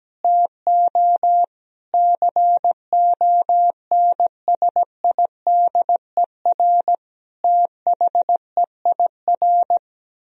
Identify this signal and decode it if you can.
Morse code, heard as a keyed tone, about 17 wpm, 700 Hz. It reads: TO CONSIDER THEIR
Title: Sentences from Top 1000 Words - Review Part 4 - 17wpm